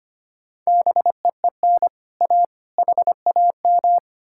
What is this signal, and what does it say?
Morse code, heard as a keyed tone, about 33 wpm, 700 Hz. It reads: BEEN A HAM